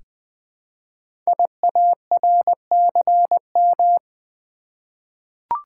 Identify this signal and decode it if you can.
Morse code, heard as a keyed tone, about 20 wpm, 700 Hz. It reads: IARCM
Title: Code Groups - 5 Letters - 20wpm